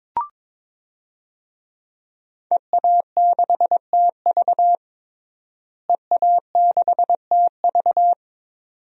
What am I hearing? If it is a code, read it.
Morse code, heard as a keyed tone, about 22 wpm, 700 Hz. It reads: EA6TV EA6TV
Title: DXCC-100 Call Signs - 5 Characters - Repeated 2x - 22wpm